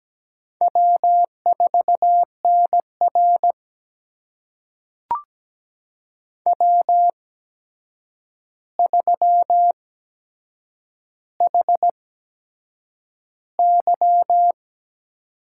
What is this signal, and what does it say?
Morse code, heard as a keyed tone, about 17 wpm, spaced as for 7 wpm, 700 Hz. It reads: W4NR W3HY
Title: USA Call Signs - 4 Characters - 8x Character Spacing - 17wpm